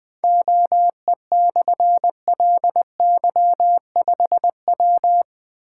Morse code, heard as a keyed tone, about 20 wpm, 700 Hz. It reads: OE/LY5W
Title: Difficult Call Signs 20wpm